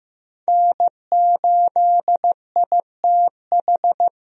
Morse code, heard as a keyed tone, about 15 wpm, 700 Hz. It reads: N8ITH
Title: USA Call Signs - 5 Characters - Repeated 3x - 15wpm